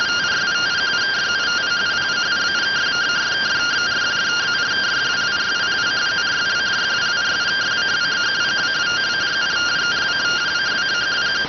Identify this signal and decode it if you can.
Two-tone teleprinter signal, about 170 Hz shift.